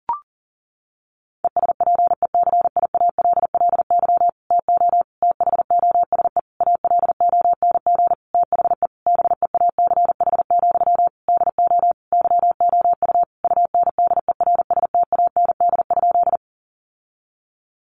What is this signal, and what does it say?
Morse code, heard as a keyed tone, about 40 wpm, 700 Hz. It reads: ESPECIALLY TO THOSE ALONG THE BEACH, DO YOU UNDERSTAND?
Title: Sentences from Top 1500 Words 40wpm